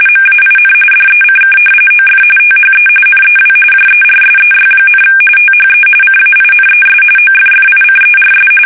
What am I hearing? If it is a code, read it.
RTTY, about 45.45 baud. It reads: MAIN DECK OF THE NEW JERSEY, THE CAINES OFFICERS
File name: fsk850 / battleshipnj